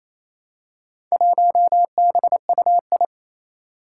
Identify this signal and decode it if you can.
Morse code, heard as a keyed tone, about 28 wpm, 700 Hz. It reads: 1BUI
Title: Four Challenging Letter-Number Combinations 28wpm